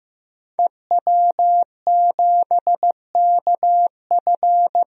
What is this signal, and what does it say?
Morse code, heard as a keyed tone, about 15 wpm, 700 Hz. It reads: EW7KF
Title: DXCC-100 Call Signs - 5 Characters - Repeated 2x - 15wpm